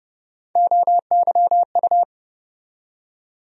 Morse code, heard as a keyed tone, about 30 wpm, 700 Hz. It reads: OYU
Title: USA Call Sign Postfixes - 3 Letters 30wpm